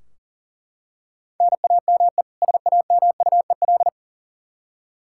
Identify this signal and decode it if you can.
Morse code, heard as a keyed tone, about 40 wpm, 700 Hz. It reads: NAME SAMUEL
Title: Practice QSO - Exchange 1 - Beginning - 40wpm